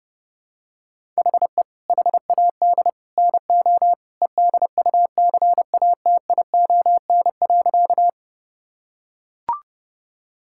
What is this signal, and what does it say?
Morse code, heard as a keyed tone, about 30 wpm, 700 Hz. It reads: HE HAD NO EDUCATION.
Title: Sentences from Top 1000 Words - Review Part 6 - 30wpm